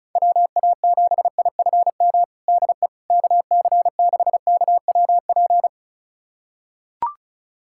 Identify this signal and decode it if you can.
Morse code, heard as a keyed tone, about 36 wpm, 700 Hz. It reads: WA7IFM DE KC6KWP
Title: USA Callsign DE Callsign - 6 Characters - 35wpm